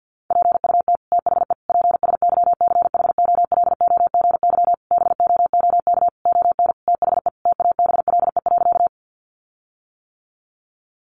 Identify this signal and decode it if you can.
Morse code, heard as a keyed tone, about 49 wpm, 700 Hz. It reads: PUT THE PSYCHOLOGY BOOK ON THE TABLE.
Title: Sentences from Top 1400 Words 50wpm